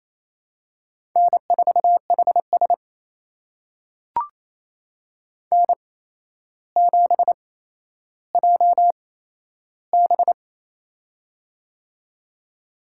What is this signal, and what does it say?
Morse code, heard as a keyed tone, about 28 wpm, spaced as for 12 wpm, 700 Hz. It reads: N4HS N7JB
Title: USA Call Signs - 4 Characters - 8x Character Spacing - 28wpm